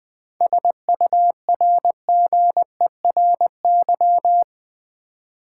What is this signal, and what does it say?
Morse code, heard as a keyed tone, about 21 wpm, 700 Hz. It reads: SURGERY
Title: Seven Letter Words - 20wpm